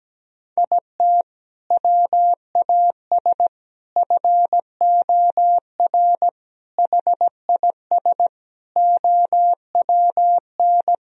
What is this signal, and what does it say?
Morse code, heard as a keyed tone, about 17 wpm, 700 Hz. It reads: IT WAS FOR HIS OWN
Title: Sentences from Top 1000 Words - Review Part 3 - 17wpm